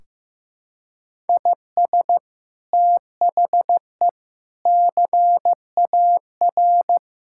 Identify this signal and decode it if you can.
Morse code, heard as a keyed tone, about 16 wpm, 700 Hz. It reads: IS THE CAR
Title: Sentences from Top 1600 Words 15wpm